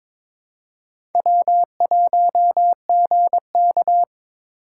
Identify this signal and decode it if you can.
Morse code, heard as a keyed tone, about 26 wpm, 700 Hz. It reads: W1GK
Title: Speed Racing - USA 1x2 Call Signs 22wpm